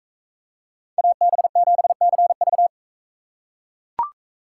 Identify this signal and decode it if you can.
Morse code, heard as a keyed tone, about 41 wpm, 700 Hz. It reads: AB7CV